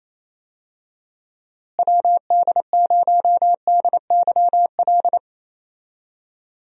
Morse code, heard as a keyed tone, about 27 wpm, 700 Hz. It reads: WD0DYL
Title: USA Call Signs - 6 Characters - Repeated 2x - 28wpm